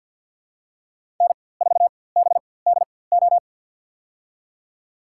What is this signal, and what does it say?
Morse code, heard as a keyed tone, about 53 wpm, spaced as for 24 wpm, 700 Hz. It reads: N4BDO
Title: USA Call Signs - 5 Characters - 4x Character Spacing - 50wpm